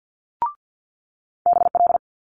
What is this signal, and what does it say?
Morse code, heard as a keyed tone, about 50 wpm, 700 Hz. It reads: BL
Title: Multi-letter Phonemes Quick Fire 50wpm